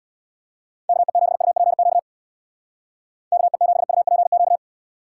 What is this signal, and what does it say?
Morse code, heard as a keyed tone, about 61 wpm, 700 Hz. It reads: KE7UPQ KE7UPQ
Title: Speed Racing - USA 2x3 Call Signs 50wpm